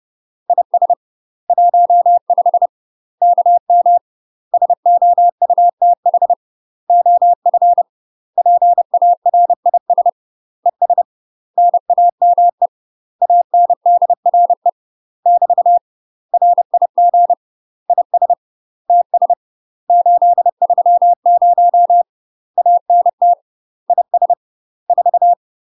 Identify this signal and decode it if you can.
Morse code, heard as a keyed tone, about 30 wpm, 700 Hz. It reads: IS 15 KM SOUTH OF PARIS ES NAME ANDRE <BT> RIG IS TS 830 ANT IS 4
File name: UK MASTER QSOs TRAINING / 30WPM / UK MASTER QSOs 0024